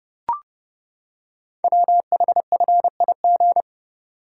Morse code, heard as a keyed tone, about 30 wpm, 700 Hz. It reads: WHFIG
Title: Code Groups - 5 Letters - 30wpm